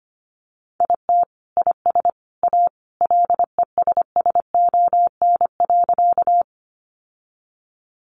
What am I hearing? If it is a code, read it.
Morse code, heard as a keyed tone, about 25 wpm, 700 Hz. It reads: IT IS A LESSON.
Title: Sentences from Top 1200 Words 25wpm